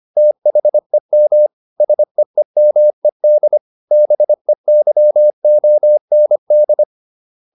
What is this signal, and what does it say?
Morse code, heard as a keyed tone, about 25 wpm, 600 Hz. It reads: THEM SEEMED BEYOND